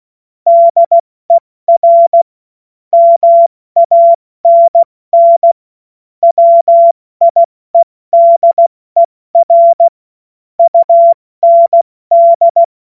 Morse code, heard as a keyed tone, about 16 wpm, 680 Hz. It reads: DER MANN WIEDER UND
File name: Morsekurs / Witze / Witz 11. Mai 2021, 70 F80